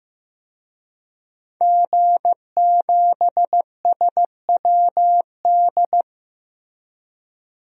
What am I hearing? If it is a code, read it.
Morse code, heard as a keyed tone, about 15 wpm, 700 Hz. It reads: G7SWD
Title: DXCC-100 Call Signs - 5 Characters - Repeated 2x - 15wpm